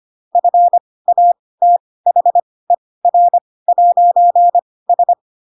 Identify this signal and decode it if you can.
Morse code, heard as a keyed tone, about 25 wpm, spaced as for 18 wpm, 700 Hz. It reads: FATHER'S